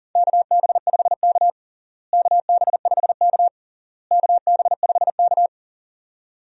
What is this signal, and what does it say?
Morse code, heard as a keyed tone, about 40 wpm, 700 Hz. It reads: KB5K KB5K KB5K
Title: USA Call Signs - 4 Characters - Repeated 3x - 40wpm